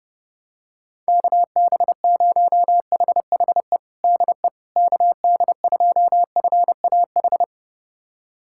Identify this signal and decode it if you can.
Morse code, heard as a keyed tone, about 30 wpm, 700 Hz. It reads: KB0HHE DE KD2FAH
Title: USA Callsign DE Callsign 30wpm